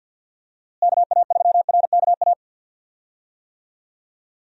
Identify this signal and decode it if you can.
Morse code, heard as a keyed tone, about 51 wpm, 700 Hz. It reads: KA3RKA
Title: USA Call Signs - 6 Characters - Repeated 2x - 50wpm